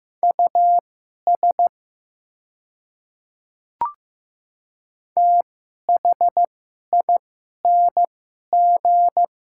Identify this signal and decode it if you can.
Morse code, heard as a keyed tone, about 15 wpm, spaced as for 11 wpm, 700 Hz. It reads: US THING
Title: 2x Character Spacing - Top 200 Words - 15wpm